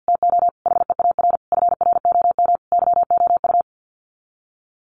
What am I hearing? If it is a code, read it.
Morse code, heard as a keyed tone, about 50 wpm, 700 Hz. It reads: TO HEAR FROM YOU
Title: Common QSO Phrases 50wpm